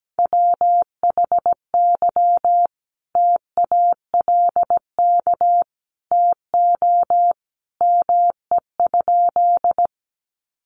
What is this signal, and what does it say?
Morse code, heard as a keyed tone, about 17 wpm, 700 Hz. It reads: WHY TALK TO ME?
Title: Sentences from Top 400 Words 17wpm